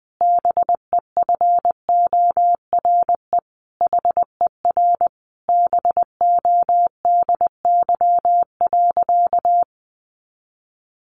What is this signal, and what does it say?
Morse code, heard as a keyed tone, about 20 wpm, 700 Hz. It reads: BEFORE HER BODY.
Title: Sentences from Top 1500 Words 20wpm